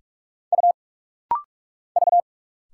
Morse code, heard as a keyed tone, about 45 wpm, 700 Hz. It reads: U V
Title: Single Letters - Rapid-Fire 45wpm